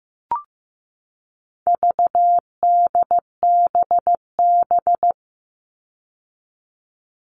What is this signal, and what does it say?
Morse code, heard as a keyed tone, about 15 wpm, 700 Hz. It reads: VDBB
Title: All Cut Numbers - 4-Digit 15wpm